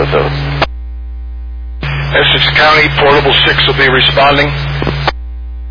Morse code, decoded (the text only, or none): none